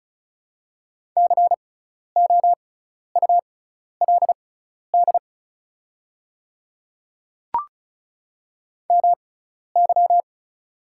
Morse code, heard as a keyed tone, about 36 wpm, spaced as for 12 wpm, 700 Hz. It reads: COULD MY
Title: 6x Character Spacing - Top 300 Words - 35wpm